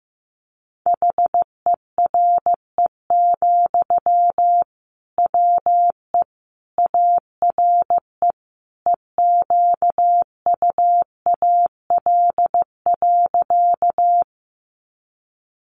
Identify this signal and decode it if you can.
Morse code, heard as a keyed tone, about 15 wpm, 700 Hz. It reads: HERE, WE ARE EQUAL.
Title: Sentences from Top 1400 Words 15wpm